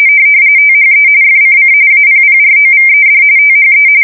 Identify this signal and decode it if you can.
RTTY, about 45.45 baud. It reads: THE FCC APPROVED THE
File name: fsk170 / teletype-RTTY